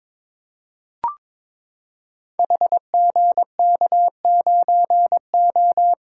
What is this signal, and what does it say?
Morse code, heard as a keyed tone, about 22 wpm, 700 Hz. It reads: HGK9O
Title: Code Groups - 5 Letter:Numbers - 22wpm